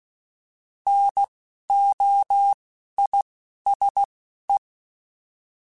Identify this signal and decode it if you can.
Morse code, heard as a keyed tone, about 16 wpm, spaced as for 11 wpm, 800 Hz. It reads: NOISE